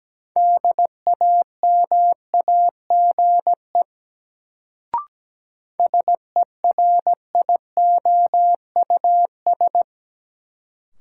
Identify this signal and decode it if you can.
Morse code, heard as a keyed tone, about 17 wpm, 700 Hz. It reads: DAMAGE SERIOUS
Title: Sentences from Top 1100 Words - New Word Drill - 17wpm